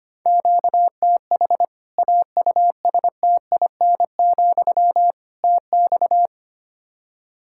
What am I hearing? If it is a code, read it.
Morse code, heard as a keyed tone, about 25 wpm, 700 Hz. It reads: QTH AUSTIN, TX
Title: Practice QSO - Exchange 1 - Beginning - 25wpm